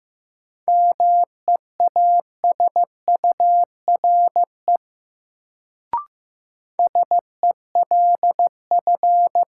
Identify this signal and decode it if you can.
Morse code, heard as a keyed tone, about 15 wpm, 700 Hz. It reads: MEASURE SELF
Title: Sentences from Top 1000 Words - Review Part 1 - New Word Drill - 15wpm